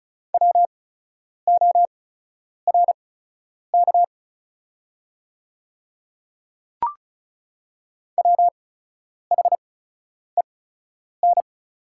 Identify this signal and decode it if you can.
Morse code, heard as a keyed tone, about 36 wpm, spaced as for 10 wpm, 690 Hz. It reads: WORK WHEN
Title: 8x Character Spacing - Top 500 Words - 35wpm